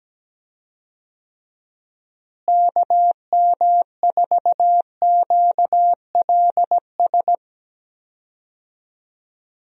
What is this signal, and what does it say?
Morse code, heard as a keyed tone, about 17 wpm, 700 Hz. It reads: KM4QLS
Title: USA Call Signs - 6 Characters - Repeated 2x - 17wpm